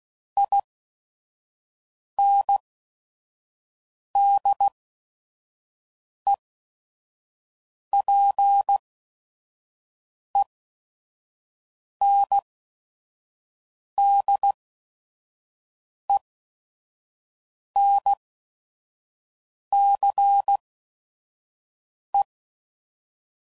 Morse code, heard as a keyed tone, about 16 wpm, spaced as for 5 wpm, 800 Hz. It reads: INDEPENDENCE
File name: CW Code Groups @5WPM / fromSM's / Cities